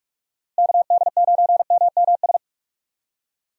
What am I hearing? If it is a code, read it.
Morse code, heard as a keyed tone, about 41 wpm, 700 Hz. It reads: KD9MMS
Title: USA Call Signs - 6 Characters - Repeated 3x - 45wpm